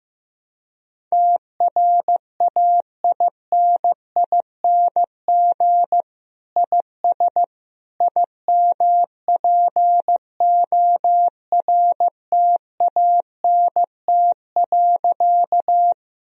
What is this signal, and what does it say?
Morse code, heard as a keyed tone, about 15 wpm, 700 Hz. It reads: TRAINING IS IMPORTANT.
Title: Sentences from Top 1000 Words - Review Part 4 - 15wpm